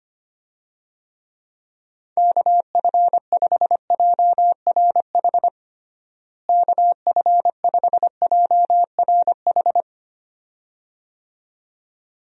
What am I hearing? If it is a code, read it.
Morse code, heard as a keyed tone, about 25 wpm, 700 Hz. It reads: KF5JRH KF5JRH
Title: USA Call Signs - 6 Characters - Repeated 2x - 25wpm